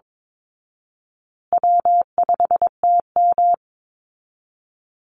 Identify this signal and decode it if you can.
Morse code, heard as a keyed tone, about 22 wpm, 700 Hz. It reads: W5TM
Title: Easy Call Signs Repeated 3x - 22wpm